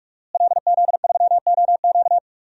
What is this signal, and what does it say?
Morse code, heard as a keyed tone, about 44 wpm, 700 Hz. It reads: LZ3OQ
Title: DXCC-100 Call Signs - 5 Characters - Repeated 3x - 45wpm